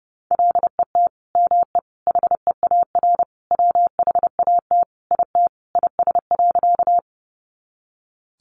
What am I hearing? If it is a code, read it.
Morse code, heard as a keyed tone, about 30 wpm, 700 Hz. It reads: LET ME HEAR WHAT IT IS.
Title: Sentences from Top 300 Words 30wpm